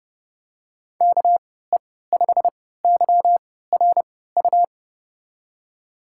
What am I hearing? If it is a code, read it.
Morse code, heard as a keyed tone, about 30 wpm, spaced as for 17 wpm, 700 Hz. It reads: KE5YRU
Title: USA Call Signs - 6 Characters - 3x Character Spacing - 30wpm